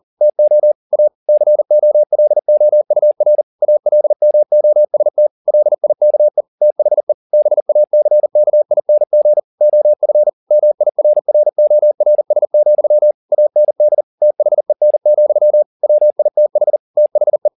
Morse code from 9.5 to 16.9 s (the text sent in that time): OF MIRRORS, AND THEN, WITH